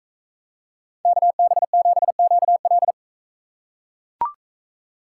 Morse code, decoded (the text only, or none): KB7QL